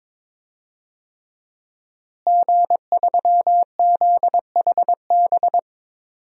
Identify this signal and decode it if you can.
Morse code, heard as a keyed tone, about 22 wpm, 700 Hz. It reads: G3ZHB
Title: DXCC-100 Call Signs - 5 Characters - Repeated 3x - 22wpm